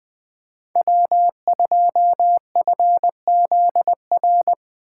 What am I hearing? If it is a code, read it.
Morse code, heard as a keyed tone, about 20 wpm, 700 Hz. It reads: W2FZR
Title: Speed Racing - USA 1x3 Call Signs 17wpm